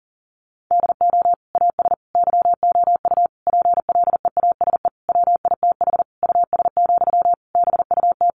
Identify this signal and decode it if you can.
Morse code, heard as a keyed tone, about 40 wpm, 700 Hz. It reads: DO AS YOU PLEASE WITH US, BUT
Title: Sentences from Top 1500 Words 40wpm